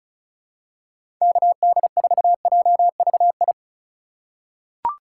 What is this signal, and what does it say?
Morse code, heard as a keyed tone, about 34 wpm, 700 Hz. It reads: KD4JVI
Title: USA Call Signs - 6 Characters - 2x Character Spacing - 35wpm